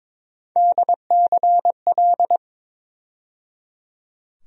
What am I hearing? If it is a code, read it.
Morse code, heard as a keyed tone, about 22 wpm, 700 Hz. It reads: DCL
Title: USA Call Sign Postfixes - 3 Letters 22wpm